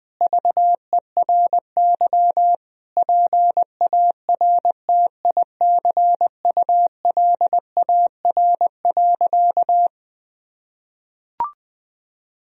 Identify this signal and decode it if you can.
Morse code, heard as a keyed tone, about 20 wpm, 700 Hz. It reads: VERY PARTICULAR.
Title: Sentences from Top 1000 Words - Review Part 4 - 20wpm